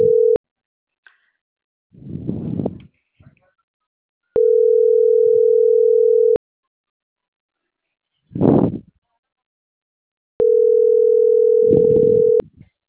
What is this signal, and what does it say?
Ringback tone (North America), 440 + 480 Hz.